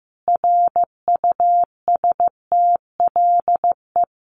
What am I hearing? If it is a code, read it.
Morse code, heard as a keyed tone, about 15 wpm, 700 Hz. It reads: RUSTLE
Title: A Sound Of Thunder 15wpm